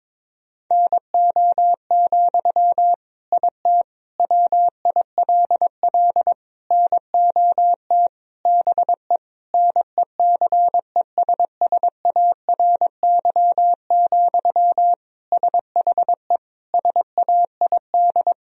Morse code, heard as a keyed tone, about 22 wpm, 700 Hz. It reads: NO, IT WILL NOT BE NECESSARY, SHE SAID
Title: Sentences from Top 1000 Words - Review Part 4 - 22wpm